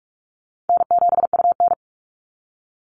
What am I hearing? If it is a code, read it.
Morse code, heard as a keyed tone, about 44 wpm, 700 Hz. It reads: N7UN
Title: NA SOTA Activator Callsigns 45wpm